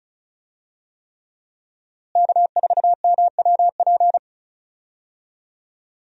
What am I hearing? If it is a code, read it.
Morse code, heard as a keyed tone, about 36 wpm, 700 Hz. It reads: K4MWP
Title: USA Call Signs - 5 Characters - Repeated 3x - 35wpm